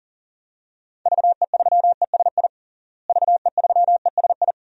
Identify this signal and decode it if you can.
Morse code, heard as a keyed tone, about 40 wpm, 700 Hz. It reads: VE3ESI VE3ESI
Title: DXCC-100 Call Signs - 6 Characters - Repeated 2x - 40wpm